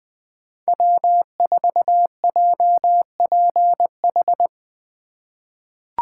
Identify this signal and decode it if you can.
Morse code, heard as a keyed tone, about 20 wpm, 700 Hz. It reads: W4JPH
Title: K1USN SST QSO Practice - 20wpm